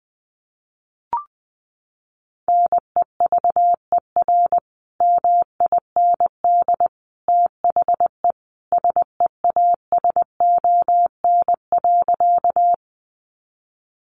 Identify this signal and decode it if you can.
Morse code, heard as a keyed tone, about 20 wpm, 700 Hz. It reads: NEVER MIND THE SEASON.
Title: Sentences from Top 1400 Words 20wpm